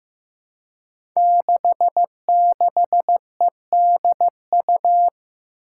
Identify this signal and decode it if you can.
Morse code, heard as a keyed tone, about 15 wpm, 700 Hz. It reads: 66EDU
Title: Five Challenging Letter-Number Combinations 15wpm